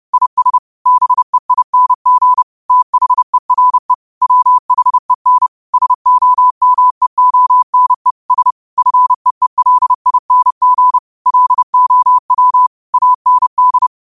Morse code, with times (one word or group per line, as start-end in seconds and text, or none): 0.1-0.6 IS
0.9-2.4 BEING
2.7-3.9 THERE
4.2-5.5 WHEN
5.7-8.1 SOMEONE
8.3-8.5 S
8.8-11.0 FEELING
11.3-12.7 LOW
12.9-13.9 AND